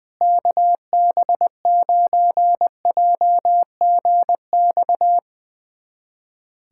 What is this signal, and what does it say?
Morse code, heard as a keyed tone, about 21 wpm, 700 Hz. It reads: KB9JGX
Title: USA Callsign DE Callsign 20wpm